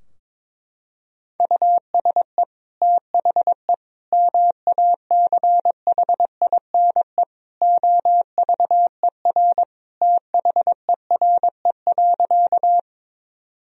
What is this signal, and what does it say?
Morse code, heard as a keyed tone, about 22 wpm, 700 Hz. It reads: USE THE MACHINE OVER THERE.
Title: Sentences from Top 1000 Words - Review Part 2 - 22wpm